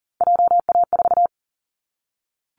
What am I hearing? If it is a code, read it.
Morse code, heard as a keyed tone, about 40 wpm, 700 Hz. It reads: JA4
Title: Callsign Prefixes for Top 25 DXCC - Repeated 3x - 40wpm